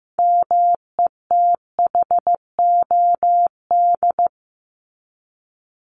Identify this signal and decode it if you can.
Morse code, heard as a keyed tone, about 15 wpm, 700 Hz. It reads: METHOD
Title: ES:S Suffix - Top 100 Words with Base Words 15wpm